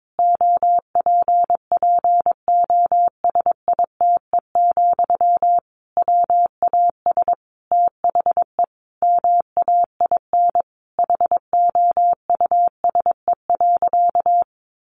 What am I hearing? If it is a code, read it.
Morse code, heard as a keyed tone, about 22 wpm, 700 Hz. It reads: OPPOSITE, WAS THE MAIN HOUSE.
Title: Sentences from Top 1300 Words 22wpm